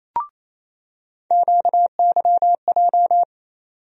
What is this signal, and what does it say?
Morse code, heard as a keyed tone, about 28 wpm, 700 Hz. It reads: QYJ
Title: USA Call Sign Postfixes - 3 Letters 28wpm